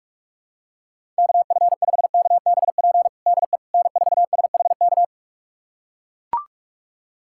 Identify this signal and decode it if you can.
Morse code, heard as a keyed tone, about 42 wpm, 700 Hz. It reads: KF5KBP DE N4SHK
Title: USA Callsign DE Callsign 45wpm